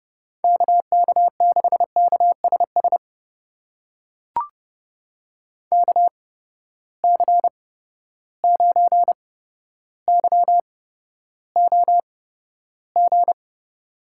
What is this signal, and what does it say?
Morse code, heard as a keyed tone, about 30 wpm, spaced as for 13 wpm, 700 Hz. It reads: KK6KSS KC9YOG
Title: USA Call Signs - 6 Characters - 8x Character Spacing - 30wpm